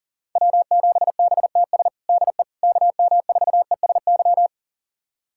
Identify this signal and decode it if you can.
Morse code, heard as a keyed tone, about 40 wpm, 700 Hz. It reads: W7BTS DE KM4ESY
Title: USA Callsign DE Callsign 40wpm